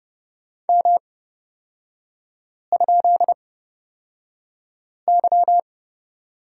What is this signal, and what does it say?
Morse code, heard as a keyed tone, about 30 wpm, spaced as for 5 wpm, 700 Hz. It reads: M?Y
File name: Lesson 36e - BT - Three Characters 30wpm